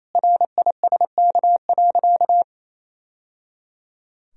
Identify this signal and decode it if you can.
Morse code, heard as a keyed tone, about 28 wpm, 700 Hz. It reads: RISK.
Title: Sentences from Top 1000 Words - Review Part 4 - 28wpm